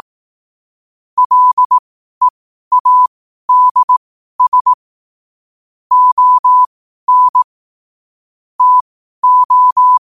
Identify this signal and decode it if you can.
Morse code, heard as a keyed tone, about 18 wpm, spaced as for 12 wpm, 1000 Hz. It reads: LEADS ON TO
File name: qotdfiles / 12wpm